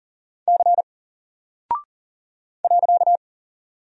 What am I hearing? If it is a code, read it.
Morse code, heard as a keyed tone, about 40 wpm, 700 Hz. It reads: C .
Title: Single Character (Letter:Number:Punctuation:Prosign) 40wpm